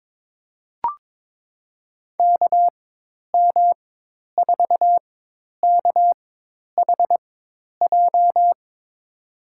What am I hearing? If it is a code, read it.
Morse code, heard as a keyed tone, about 22 wpm, spaced as for 10 wpm, 700 Hz. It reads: KM4KHJ
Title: USA Call Signs - 6 Characters - 4x Character Spacing - 22wpm